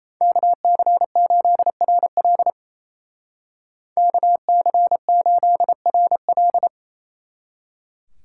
Morse code, heard as a keyed tone, about 36 wpm, 700 Hz. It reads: KC8RL KC8RL